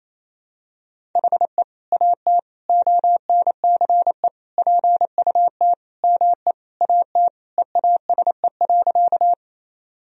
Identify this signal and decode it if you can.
Morse code, heard as a keyed tone, about 28 wpm, 700 Hz. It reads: HE AT ONCE PUT ME AT EASE.
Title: Sentences from Top 1000 Words - Review Part 1 - 28wpm